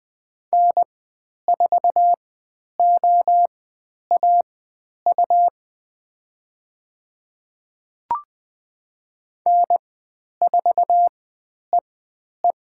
Morse code, heard as a keyed tone, about 20 wpm, spaced as for 10 wpm, 700 Hz. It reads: N4OAU N4EE
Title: 102 Lesson 17g - Letter u - Call Signs 20wpm